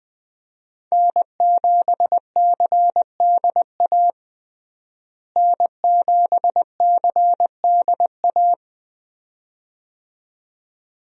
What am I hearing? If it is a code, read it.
Morse code, heard as a keyed tone, about 20 wpm, 700 Hz. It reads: N7CDA N7CDA
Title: USA Call Signs - 5 Characters - Repeated 2x - 20wpm